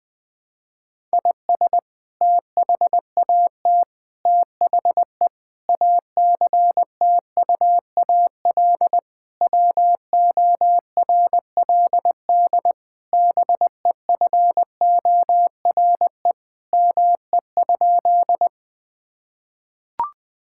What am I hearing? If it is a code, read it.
Morse code, heard as a keyed tone, about 20 wpm, 700 Hz. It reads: IS THAT THE ACTUAL WORLD BEFORE ME?
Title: Sentences from Top 1000 Words - Review Part 5 - 20wpm